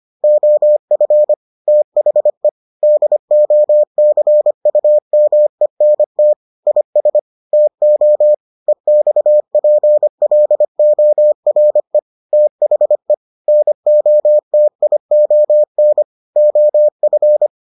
Morse code, heard as a keyed tone, about 25 wpm, 600 Hz. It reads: OF THE DOCUMENT IS TO EXPLORE THE NOTION OF